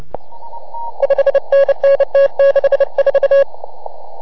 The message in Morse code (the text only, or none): 5NN T64